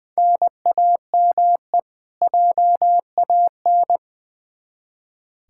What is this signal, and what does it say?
Morse code, heard as a keyed tone, about 20 wpm, 700 Hz. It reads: NAME JAN
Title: Practice QSO - Exchange 1 - Beginning - 20wpm